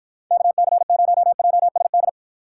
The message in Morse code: KC0JSD

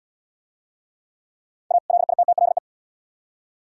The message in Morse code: A LITTLE

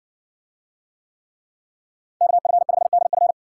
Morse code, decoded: KF5DF

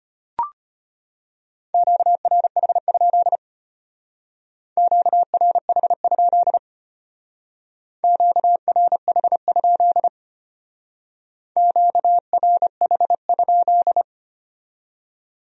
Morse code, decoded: QRH? QRH? QRH? QRH?

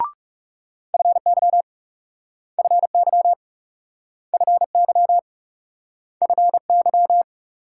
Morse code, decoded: FY FY FY FY

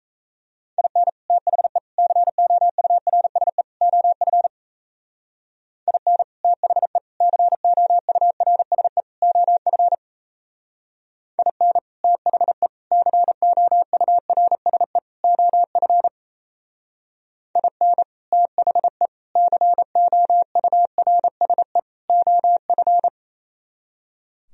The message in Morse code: IN THE COURSE OF IN THE COURSE OF IN THE COURSE OF IN THE COURSE OF

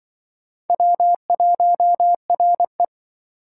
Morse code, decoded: W1RE

